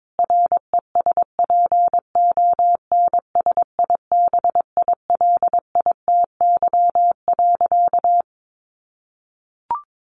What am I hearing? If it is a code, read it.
Morse code, heard as a keyed tone, about 22 wpm, 700 Hz. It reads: RESPONSIBILITY.
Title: Sentences from Top 1500 Words 22wpm